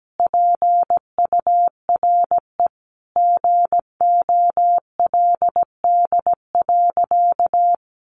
Morse code, heard as a keyed tone, about 17 wpm, 700 Hz. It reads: PURE GOLD.
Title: Sentences from Top 1500 Words 17wpm